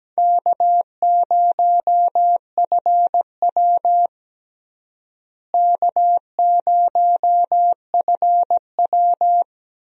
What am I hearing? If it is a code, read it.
Morse code, heard as a keyed tone, about 17 wpm, 700 Hz. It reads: K0FW K0FW
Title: USA Call Signs - 4 Characters - Repeated 2x - 17wpm